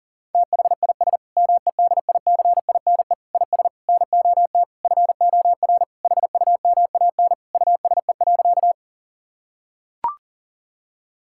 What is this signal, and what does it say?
Morse code, heard as a keyed tone, about 40 wpm, 690 Hz. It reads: THIS MEDICINE IS NOT FOR HUMAN USE.
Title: Sentences from Top 1000 Words - Review Part 6 - 40wpm